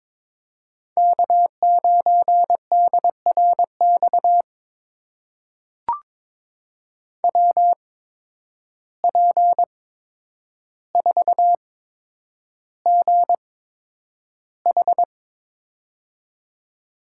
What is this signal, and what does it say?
Morse code, heard as a keyed tone, about 22 wpm, spaced as for 9 wpm, 700 Hz. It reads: K9DRX WP4GH
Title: USA Call Signs - 8x Character Spacing - 22wpm